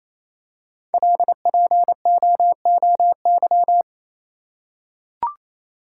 Morse code, heard as a keyed tone, about 28 wpm, 700 Hz. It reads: LPOOY
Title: Code Groups - 5 Letters - 28wpm